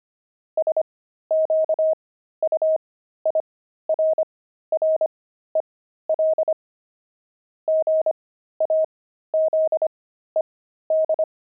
Morse code, heard as a keyed tone, about 25 wpm, spaced as for 13 wpm, 630 Hz. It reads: SQUIRREL GAZED
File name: pangram-fs-13-cs-25-squirrel